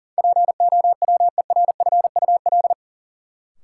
Morse code, heard as a keyed tone, about 40 wpm, 700 Hz. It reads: POWERFUL
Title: Sentences from Top 1300 Words - New Word Drill Alt - 40wpm